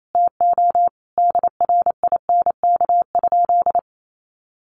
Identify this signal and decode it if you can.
Morse code, heard as a keyed tone, about 28 wpm, 700 Hz. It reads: TO DRINK?
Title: Sentences from Top 1500 Words 28wpm